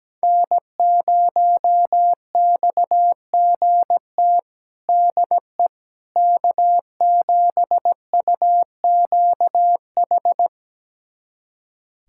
Morse code, heard as a keyed tone, about 17 wpm, 700 Hz. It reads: N0XGT DE K7UQH
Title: USA Callsign DE Callsign - 5 Characters - 17wpm